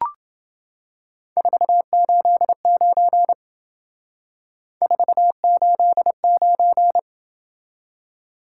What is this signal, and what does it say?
Morse code, heard as a keyed tone, about 28 wpm, 700 Hz. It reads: 489 489